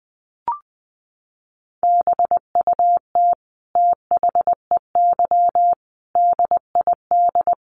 Morse code, heard as a keyed tone, about 20 wpm, 700 Hz. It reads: BUT THEY DID
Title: Sentences from Top 1400 Words 20wpm